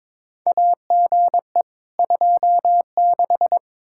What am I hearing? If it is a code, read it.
Morse code, heard as a keyed tone, about 22 wpm, 700 Hz. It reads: AGE 26
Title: Practice QSO - Exchange 3 - More Info - 22wpm